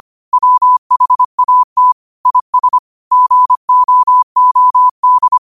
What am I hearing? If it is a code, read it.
Morse code, heard as a keyed tone, about 25 wpm, 1000 Hz. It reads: WHAT IS GOOD
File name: 25wpm